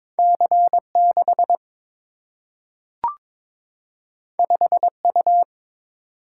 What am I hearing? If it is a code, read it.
Morse code, heard as a keyed tone, about 22 wpm, 700 Hz. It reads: C 6 5 U
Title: Single-Letter Single-Number 22wpm